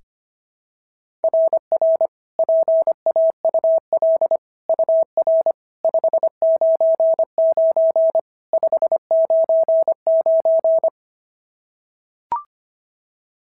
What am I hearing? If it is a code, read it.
Morse code, heard as a keyed tone, about 23 wpm, 650 Hz. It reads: RR PAUL UR 599 599
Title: SOTA QSO Practice 25wpm